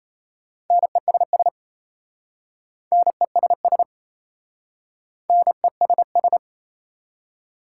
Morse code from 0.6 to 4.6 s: NESS NESS

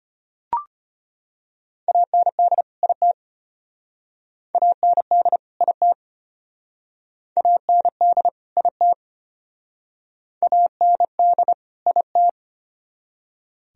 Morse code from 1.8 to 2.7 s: AND

